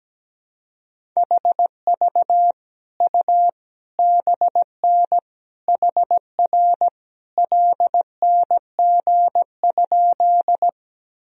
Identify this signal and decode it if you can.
Morse code, heard as a keyed tone, about 17 wpm, 700 Hz. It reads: HV U BN HR LNG?
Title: Sentences from Top 200 Words and Abbreviations 17wpm